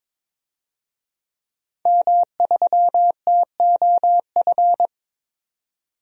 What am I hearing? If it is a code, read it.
Morse code, heard as a keyed tone, about 22 wpm, 700 Hz. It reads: M3TOF
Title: DXCC-100 Call Signs - 5 Characters - 22wpm